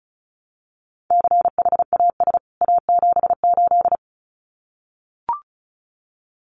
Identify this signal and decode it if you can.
Morse code, heard as a keyed tone, about 34 wpm, 700 Hz. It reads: CHAS A78
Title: CWT Exchange 35wpm